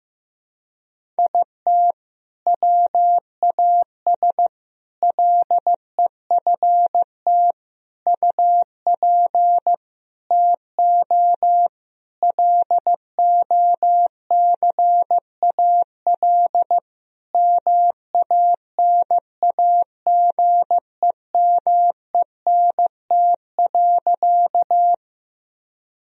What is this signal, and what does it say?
Morse code, heard as a keyed tone, about 15 wpm, 700 Hz. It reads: IT WAS LEFT UP TO LOCAL MANAGEMENT.
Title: Sentences from Top 1000 Words - Review Part 4 - 15wpm